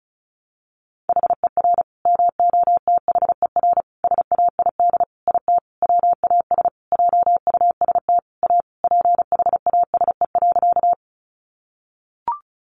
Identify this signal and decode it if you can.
Morse code, heard as a keyed tone, about 34 wpm, 700 Hz. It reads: HER MOTHER SAID IT WAS JUST A PHASE.
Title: Sentences from Top 1200 Words 35wpm